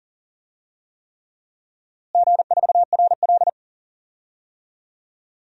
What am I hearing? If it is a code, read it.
Morse code, heard as a keyed tone, about 40 wpm, 700 Hz. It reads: G4RL
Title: DXCC-100 Call Signs - 4 Characters - Repeated 3x - 40wpm